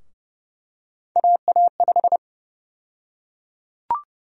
Morse code, heard as a keyed tone, about 30 wpm, 700 Hz. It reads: AA5
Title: USA Call Sign Prefixes - Repeated 2x - 30wpm